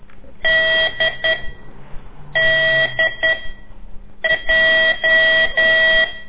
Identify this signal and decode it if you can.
Morse code, heard as a keyed tone, about 10 wpm, 1950 Hz. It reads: D D J
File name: CORSI CW